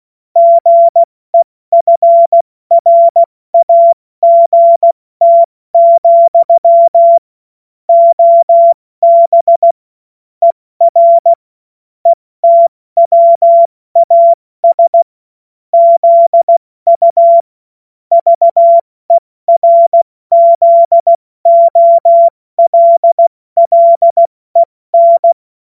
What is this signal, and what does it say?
Morse code, heard as a keyed tone, about 16 wpm, 680 Hz. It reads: GEFRAGT, OB ER ETWAS ZU VERZOLLEN
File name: Morsekurs / Witze / Witz 27. Januar 2026 70 F80